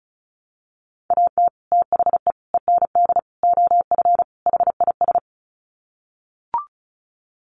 Morse code, heard as a keyed tone, about 34 wpm, 700 Hz. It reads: AT THE END OF HIS
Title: Sets of 5 Words Encore 35wpm